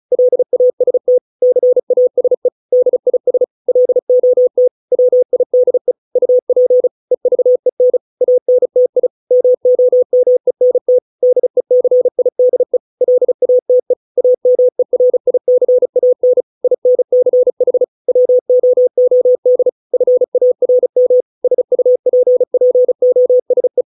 Morse code, heard as a keyed tone, about 34 wpm, 500 Hz. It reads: LAST CASE DIS LOT WIDE UP EVEN ANTI MOMENT DECIDE LATE AMERICAN INCH WOOD FARM SUPPOSE